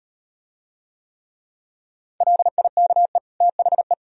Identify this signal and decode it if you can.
Morse code, heard as a keyed tone, about 38 wpm, 700 Hz. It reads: LIKE THE